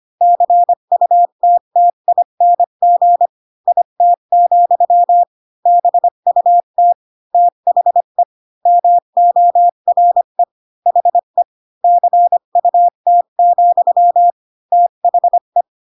Morse code, heard as a keyed tone, about 25 wpm, 700 Hz. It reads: CUTTING IT, BUT THE MORE HE CUT, THE
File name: Pinocchio / Pinocchio ch3 20 25 ik1hge